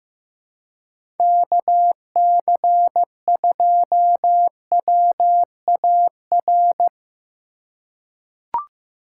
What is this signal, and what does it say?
Morse code, heard as a keyed tone, about 15 wpm, 700 Hz. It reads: KC2WAR
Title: USA Call Signs - 6 Characters - 4x Character Spacing - 15wpm